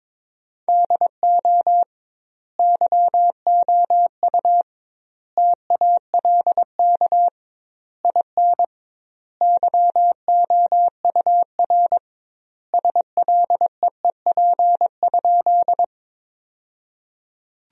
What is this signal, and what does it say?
Morse code, heard as a keyed tone, about 22 wpm, 700 Hz. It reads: DO YOU TALK IN YOUR SLEEP?
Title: 2x Word-Spacing - Sentences from Top 500 Words - 22wpm